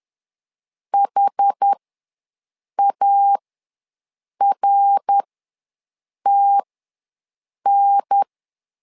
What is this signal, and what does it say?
Morse code, heard as a keyed tone, about 10 wpm, spaced as for 6 wpm, 800 Hz. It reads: HARTN
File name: LEZIONI / 3 MWRO / 3 MWR ws 10wpm 2151